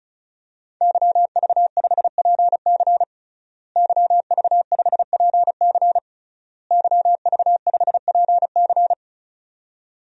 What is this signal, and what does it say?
Morse code, heard as a keyed tone, about 36 wpm, 700 Hz. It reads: YV5PC YV5PC YV5PC
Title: DXCC-100 Call Signs - 5 Characters - Repeated 3x - 35wpm